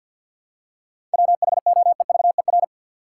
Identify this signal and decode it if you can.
Morse code, heard as a keyed tone, about 47 wpm, 700 Hz. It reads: WHOEVER
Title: Seven Letter Words - 50wpm